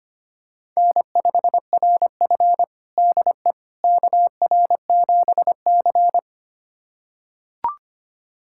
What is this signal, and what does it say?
Morse code, heard as a keyed tone, about 25 wpm, 700 Hz. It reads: N5RF DE KR7C
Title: USA Callsign DE Callsign - 4 Characters - 25wpm